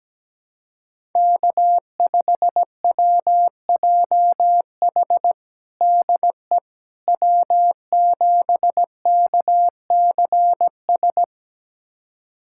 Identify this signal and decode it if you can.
Morse code, heard as a keyed tone, about 17 wpm, 700 Hz. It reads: K5WJH DE W7KCS
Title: USA Callsign DE Callsign - 5 Characters - 17wpm